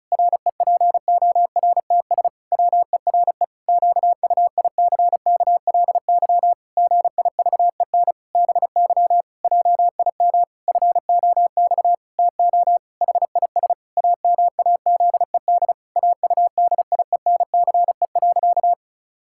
Morse code, heard as a keyed tone, about 36 wpm, 700 Hz. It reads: REPORTS WERE QUICKLY GIVEN BY JIM FOX TO HIS AMAZED AUDIENCE.
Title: Pangram Challenge v2 - 35wpm